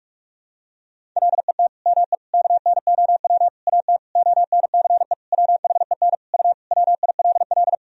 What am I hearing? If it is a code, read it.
Morse code, heard as a keyed tone, about 46 wpm, 700 Hz. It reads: LET ME KNOW AT ONCE WHEN U WILL
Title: Sentences from Top 300 Words and Abbreviations 45wpm